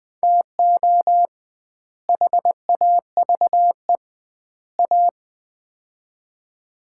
Morse code, heard as a keyed tone, about 21 wpm, 700 Hz. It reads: TO HAVE A
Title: 2x Word-Spacing - Sets of 3 Words - 20wpm